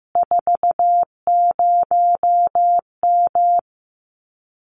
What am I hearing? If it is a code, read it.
Morse code, heard as a keyed tone, about 15 wpm, 700 Hz. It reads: 40M
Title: Common QSO Phrases Encore 15wpm